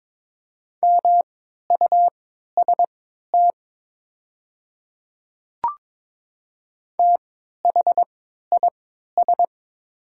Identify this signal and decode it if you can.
Morse code, heard as a keyed tone, about 22 wpm, spaced as for 12 wpm, 700 Hz. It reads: MUST THIS
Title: 3x Character Spacing - Top 200 Words - 22wpm